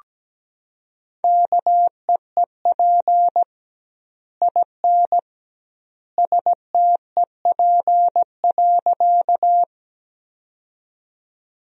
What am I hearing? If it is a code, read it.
Morse code, heard as a keyed tone, about 17 wpm, 700 Hz. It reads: KEEP IN STEP.
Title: 2x Word-Spacing - Sentences from Top 400 Words - 17wpm